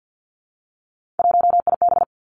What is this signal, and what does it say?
Morse code, heard as a keyed tone, about 50 wpm, 700 Hz. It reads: 1IB
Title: Three Challenging Letter-Number Combinations 50wpm